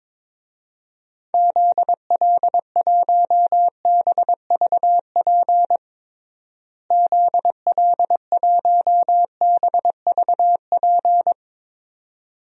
Morse code, heard as a keyed tone, about 22 wpm, 700 Hz. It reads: ZL1BVP ZL1BVP
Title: DXCC-100 Call Signs - 6 Characters - Repeated 2x - 22wpm